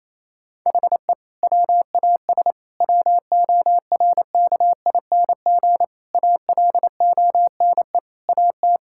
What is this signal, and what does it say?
Morse code, heard as a keyed tone, about 28 wpm, 700 Hz. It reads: HE WAS WORKING ALONE AT
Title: Sentences from Top 1000 Words - Review Part 5 - 28wpm